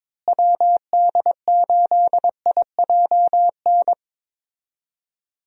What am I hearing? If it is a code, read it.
Morse code, heard as a keyed tone, about 22 wpm, 700 Hz. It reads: WD8IJN
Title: USA Call Signs - 6 Characters - Repeated 2x - 22wpm